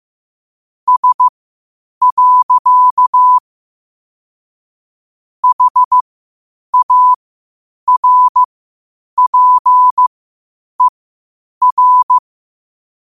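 Morse code, heard as a keyed tone, about 15 wpm, spaced as for 8 wpm, 1000 Hz. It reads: S. HARPER
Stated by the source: Morse code quotes at 8 words per minute from QOTD at 8 WPM.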